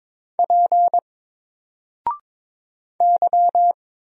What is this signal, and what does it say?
Morse code, heard as a keyed tone, about 22 wpm, 700 Hz. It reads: P Y
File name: Single-Letter-Number-22wpm